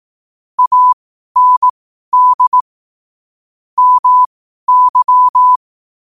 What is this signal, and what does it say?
Morse code, heard as a keyed tone, about 18 wpm, spaced as for 12 wpm, 1000 Hz. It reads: AND MY
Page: New quotes every day in morse code at 12 Words per minute.